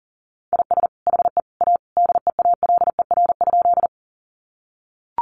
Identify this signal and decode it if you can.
Morse code, heard as a keyed tone, about 37 wpm, 700 Hz. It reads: IS HE A DEALER?
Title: Sentences from Top 1100 Words 40wpm